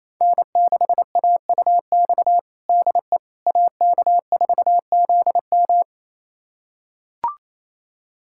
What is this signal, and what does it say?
Morse code, heard as a keyed tone, about 27 wpm, 700 Hz. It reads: N6AUX DE AK4ZM
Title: USA Callsign DE Callsign - 5 Characters - 28wpm